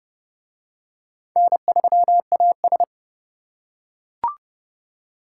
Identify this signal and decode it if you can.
Morse code, heard as a keyed tone, about 28 wpm, 700 Hz. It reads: N3AS